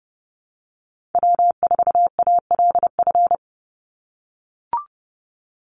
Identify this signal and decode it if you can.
Morse code, heard as a keyed tone, about 30 wpm, 700 Hz. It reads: W4ALF
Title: Call Signs - 30wpm